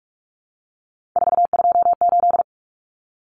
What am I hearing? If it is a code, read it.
Morse code, heard as a keyed tone, about 44 wpm, 700 Hz. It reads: 428
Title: Common Cut Numbers - 3-Digit 45wpm